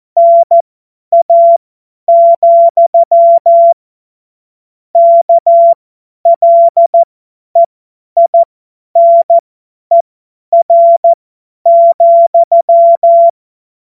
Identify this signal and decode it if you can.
Morse code, heard as a keyed tone, about 14 wpm, spaced as for 10 wpm, 680 Hz. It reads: NA, KLEINER,
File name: Morsekurs / Witze / Witz 01. März 2023 50 F70